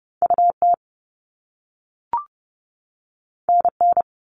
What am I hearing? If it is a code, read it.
Morse code, heard as a keyed tone, about 30 wpm, 700 Hz. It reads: UT NN
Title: All Cut Numbers - 2-Digit 30wpm